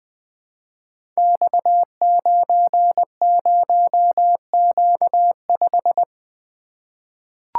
Morse code, heard as a keyed tone, about 20 wpm, 700 Hz. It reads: X90Q5
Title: Code Groups - 5 Letter:Numbers - 20wpm